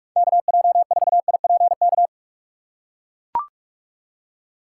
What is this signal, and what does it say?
Morse code, heard as a keyed tone, about 43 wpm, 700 Hz. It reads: KJ4IPK
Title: USA Call Signs - 6 Characters - 3x Character Spacing - 45wpm